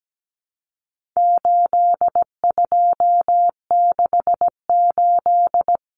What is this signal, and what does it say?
Morse code, heard as a keyed tone, about 17 wpm, 700 Hz. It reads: 8268
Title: Numbers - 4-Digit 17wpm